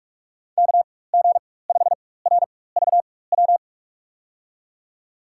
Morse code, heard as a keyed tone, about 46 wpm, spaced as for 21 wpm, 700 Hz. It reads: KG5RVW
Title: USA Call Signs - 6 Characters - 4x Character Spacing - 45wpm